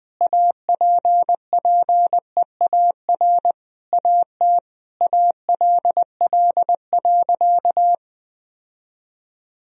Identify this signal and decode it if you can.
Morse code, heard as a keyed tone, about 21 wpm, 700 Hz. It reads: APPEAR AT ALL.
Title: Sentences from Top 1000 Words - Review Part 2 - 20wpm